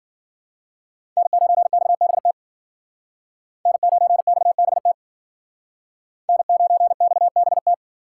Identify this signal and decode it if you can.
Morse code, heard as a keyed tone, about 52 wpm, 700 Hz. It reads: N9XBT N9XBT N9XBT